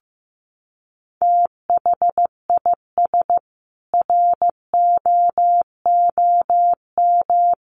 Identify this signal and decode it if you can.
Morse code, heard as a keyed tone, about 15 wpm, 700 Hz. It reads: THIS ROOM
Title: Sentences from Top 700 Words 15wpm